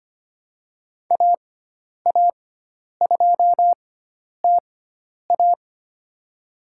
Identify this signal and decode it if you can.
Morse code, heard as a keyed tone, about 25 wpm, spaced as for 10 wpm, 700 Hz. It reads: AA2TA
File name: Lesson 16g - Letter c - Call Signs 25wpm